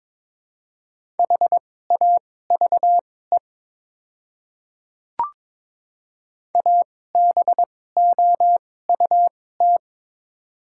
Morse code, heard as a keyed tone, about 22 wpm, spaced as for 16 wpm, 700 Hz. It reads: HAVE ABOUT
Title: 2x Character Spacing - Top 100 Words - 22wpm